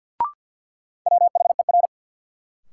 Morse code, heard as a keyed tone, about 51 wpm, 700 Hz. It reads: WHER
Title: Sentences from Top 1500 Words - New Word Drill Alt - 50wpm